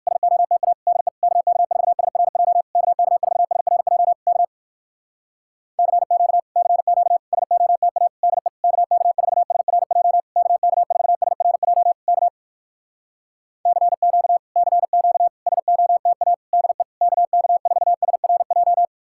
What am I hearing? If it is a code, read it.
Morse code, heard as a keyed tone, about 66 wpm, 700 Hz. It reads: SOTA DE KK4SRJ KK4SRJ K CQ CQ SOTA DE KK4SRJ KK4SRJ K CQ CQ SOTA DE KK4SRJ